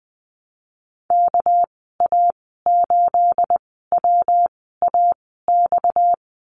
Morse code, heard as a keyed tone, about 20 wpm, spaced as for 14 wpm, 700 Hz. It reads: KA8WAX
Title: 238 Lesson 38g - Letter q - Call Signs 20wpm